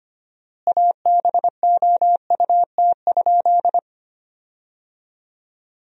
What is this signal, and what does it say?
Morse code, heard as a keyed tone, about 25 wpm, 700 Hz. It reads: ABOUT?
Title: 4x Word-Spacing - Sentences from Top 100 Words - 25wpm